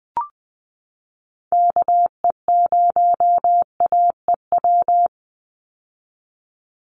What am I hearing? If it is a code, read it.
Morse code, heard as a keyed tone, about 20 wpm, 700 Hz. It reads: KE0AEW
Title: USA Call Signs 20wpm